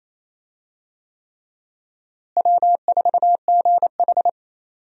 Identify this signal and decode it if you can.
Morse code, heard as a keyed tone, about 28 wpm, 700 Hz. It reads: W4GH
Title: USA Call Signs - 4 Characters - Repeated 3x - 28wpm